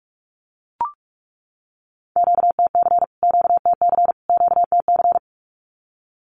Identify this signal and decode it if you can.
Morse code, heard as a keyed tone, about 44 wpm, 700 Hz. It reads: QTC QTC QTC
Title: Q-Codes Repeated 3x 45wpm